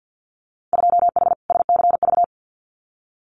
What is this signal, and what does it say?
Morse code, heard as a keyed tone, about 50 wpm, 700 Hz. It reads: 2H SCV
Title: Winter Field Day Exchanges Encore 50wpm